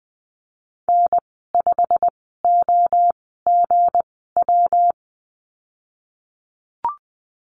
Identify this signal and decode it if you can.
Morse code, heard as a keyed tone, about 20 wpm, spaced as for 14 wpm, 700 Hz. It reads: N5OGW
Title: 245 Lesson 39g - Letter z - Call Signs 20wpm